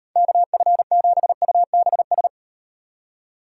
Morse code, heard as a keyed tone, about 36 wpm, 700 Hz. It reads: KF7UBS